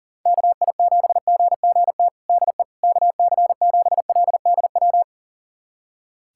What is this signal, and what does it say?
Morse code, heard as a keyed tone, about 40 wpm, 700 Hz. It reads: KI7GGT DE KC7LDW
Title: USA Callsign DE Callsign - 6 Characters - 40wpm